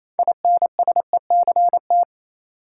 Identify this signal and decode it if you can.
Morse code, heard as a keyed tone, about 28 wpm, 700 Hz. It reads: INSECT
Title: Six Letter Words - 28wpm